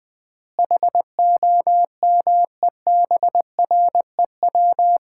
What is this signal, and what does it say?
Morse code, heard as a keyed tone, about 20 wpm, 700 Hz. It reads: HOMEBREW
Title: Common QSO Phrases - 3x Word-Spacing - 20wpm